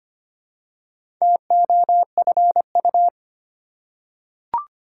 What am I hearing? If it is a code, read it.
Morse code, heard as a keyed tone, about 25 wpm, 700 Hz. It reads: TOFU
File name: practice / 25wpm / Difficult Words with Hints - 25wpm